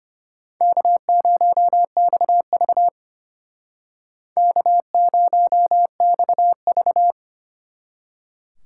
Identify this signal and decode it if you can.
Morse code, heard as a keyed tone, about 33 wpm, 700 Hz. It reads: K0XV K0XV